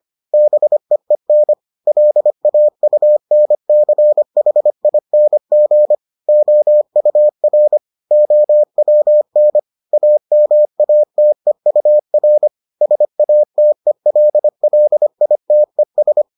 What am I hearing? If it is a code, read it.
Morse code, heard as a keyed tone, about 25 wpm, 600 Hz. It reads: BEEN LAUNCHING OUR OWN AMATEUR SATELLITES